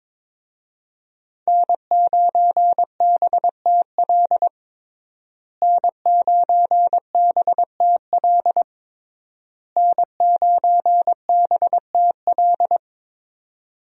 Morse code, heard as a keyed tone, about 22 wpm, 700 Hz. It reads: N9BTL N9BTL N9BTL
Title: USA Call Signs - 5 Characters - Repeated 3x - 22wpm